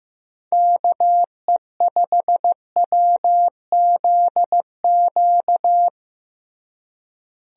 Morse code, heard as a keyed tone, about 15 wpm, 700 Hz. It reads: KE5WZQ
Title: USA Call Signs - 6 Characters - Repeated 2x - 15wpm